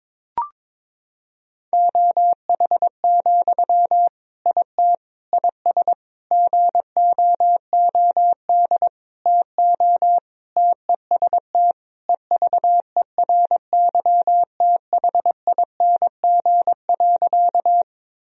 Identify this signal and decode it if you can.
Morse code, heard as a keyed tone, about 21 wpm, 700 Hz. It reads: OH, IT IS GOOD TO TEST EVERYTHING.
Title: Sentences from Top 1000 Words - Review Part 3 - 22wpm